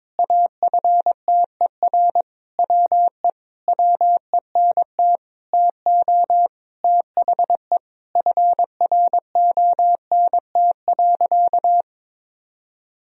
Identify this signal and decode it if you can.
Morse code, heard as a keyed tone, about 22 wpm, 700 Hz. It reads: AFTER WE WENT TO THE FRONT.
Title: Sentences from Top 1000 Words - Review Part 2 - 22wpm